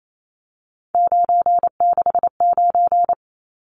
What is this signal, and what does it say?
Morse code, heard as a keyed tone, about 28 wpm, 700 Hz. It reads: 969
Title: Numbers - 3-Digit 28wpm